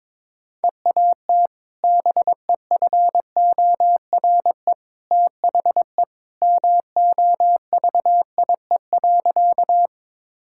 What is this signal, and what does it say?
Morse code, heard as a keyed tone, about 22 wpm, 700 Hz. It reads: EAT BEFORE THE MOVIE.
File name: practice / 22wpm / Sentences with 2-Word Phrases - 22wpm